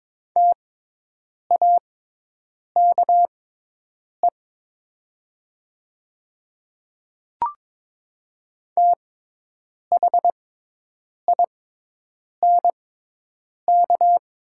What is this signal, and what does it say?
Morse code, heard as a keyed tone, about 22 wpm, spaced as for 8 wpm, 700 Hz. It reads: TAKE THINK
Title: 6x Character Spacing - Top 200 Words - 22wpm